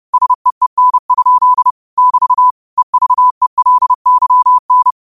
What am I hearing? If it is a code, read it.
Morse code, heard as a keyed tone, about 30 wpm, 1000 Hz. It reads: SEEN? <BT> EVELYN
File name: qotdfiles / 30wpm